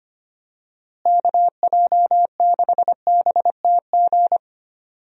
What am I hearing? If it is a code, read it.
Morse code, heard as a keyed tone, about 25 wpm, 700 Hz. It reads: KJ6BTG
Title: USA Call Signs - 6 Characters - Repeated 3x - 25wpm